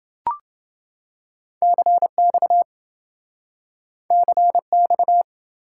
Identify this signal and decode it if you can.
Morse code, heard as a keyed tone, about 28 wpm, 700 Hz. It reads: CX CX